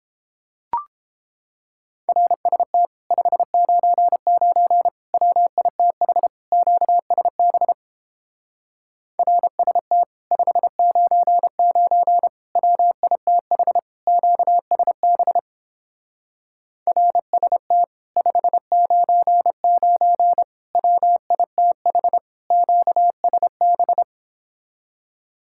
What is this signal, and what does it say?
Morse code, heard as a keyed tone, about 31 wpm, 700 Hz. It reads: RST 599 WITH QSB RST 599 WITH QSB RST 599 WITH QSB